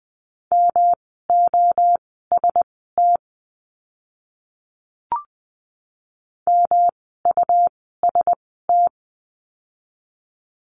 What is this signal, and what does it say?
Morse code, heard as a keyed tone, about 20 wpm, spaced as for 15 wpm, 700 Hz. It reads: MOST MUST
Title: 237 Lesson 38f - Letter q - Words 20wpm